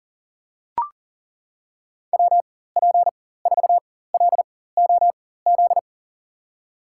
Morse code, heard as a keyed tone, about 40 wpm, spaced as for 19 wpm, 700 Hz. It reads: WP4LOZ
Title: USA Call Signs - 6 Characters - 4x Character Spacing - 40wpm